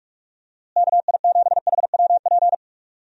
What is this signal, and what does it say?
Morse code, heard as a keyed tone, about 46 wpm, 700 Hz. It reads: KI7HWP
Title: USA Call Signs - 6 Characters - Repeated 3x - 45wpm